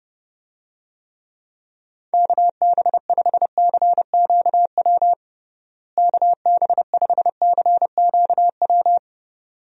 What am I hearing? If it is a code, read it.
Morse code, heard as a keyed tone, about 32 wpm, 700 Hz. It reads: KB5CQW KB5CQW
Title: USA Call Signs - 6 Characters - Repeated 2x - 30wpm